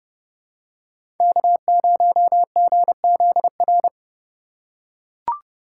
Morse code, heard as a keyed tone, about 32 wpm, 700 Hz. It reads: K0GZR
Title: USA Call Signs - 5 Characters - 2x Character Spacing - 30wpm